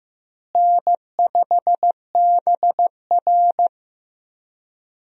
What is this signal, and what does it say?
Morse code, heard as a keyed tone, about 15 wpm, 700 Hz. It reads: N5BR
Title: USA Call Signs - 4 Characters - Repeated 3x - 15wpm